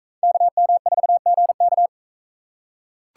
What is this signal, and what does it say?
Morse code, heard as a keyed tone, about 41 wpm, 700 Hz. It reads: KM4GK